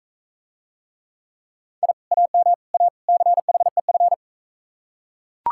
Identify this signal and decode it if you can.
Morse code, heard as a keyed tone, about 43 wpm, 700 Hz. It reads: I AM A CHEF